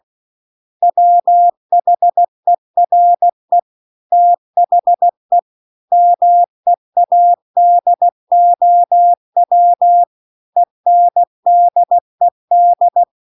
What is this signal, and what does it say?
Morse code, heard as a keyed tone, about 16 wpm, 700 Hz. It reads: WHERE THE MEADOW ENDED
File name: CW-The-Big-Twoheaded-River-2-16WPM